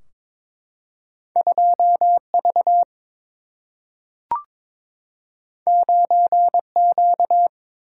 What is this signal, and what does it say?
Morse code, heard as a keyed tone, about 22 wpm, 700 Hz. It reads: 2V 9Q
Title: Single-Letter Single-Number 22wpm